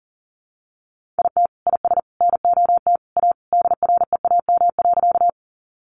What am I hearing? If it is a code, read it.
Morse code, heard as a keyed tone, about 40 wpm, 700 Hz. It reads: IT IS NOT A DREAM.
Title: Sentences from Top 1500 Words 40wpm